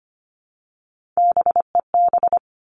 Morse code, heard as a keyed tone, about 25 wpm, 700 Hz. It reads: BEB
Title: Three Challenging Letter-Number Combinations 25wpm